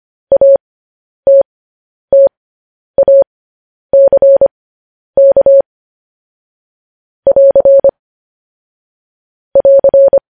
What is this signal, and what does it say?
Morse code, heard as a keyed tone, about 25 wpm, spaced as for 10 wpm, 550 Hz. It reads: ATTACK <AR> <AR>
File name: podcast / News headlines-1x-25x10